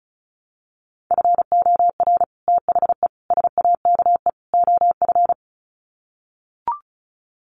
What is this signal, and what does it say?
Morse code, heard as a keyed tone, about 34 wpm, 700 Hz. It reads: FOR THE SAKE OF
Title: Sets of 4 Words 35wpm